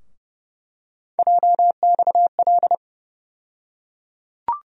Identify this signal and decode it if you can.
Morse code, heard as a keyed tone, about 29 wpm, 700 Hz. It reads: JXL
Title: USA Call Sign Postfixes - 3 Letters 30wpm